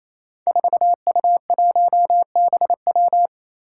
Morse code, heard as a keyed tone, about 29 wpm, 700 Hz. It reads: 4U1BW
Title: Five Challenging Letter-Number Combinations 28wpm